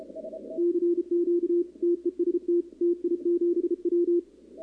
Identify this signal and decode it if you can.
Morse code, heard as a keyed tone, about 32 wpm, 340 Hz, with radio noise. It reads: CQ TEST TI7W
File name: contests / 2019-cw-wpx-kt5j-k5pi- -2330 0-11